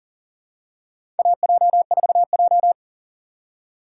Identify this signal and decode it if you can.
Morse code, heard as a keyed tone, about 40 wpm, 700 Hz. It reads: AJ4J
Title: USA Call Signs - 4 Characters - 4x Character Spacing - 40wpm